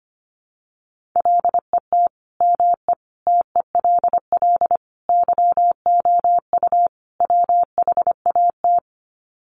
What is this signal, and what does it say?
Morse code, heard as a keyed tone, about 25 wpm, 700 Hz. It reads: LET ME TELL YOU WHAT
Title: Sets of 5 Words Encore 25wpm